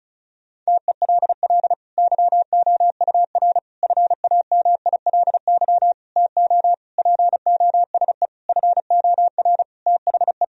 Morse code, heard as a keyed tone, about 34 wpm, 700 Hz. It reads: TELL YOUR FAMILY TO POSE FOR THE
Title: Sentences from Top 1000 Words - Review Part 2 - 35wpm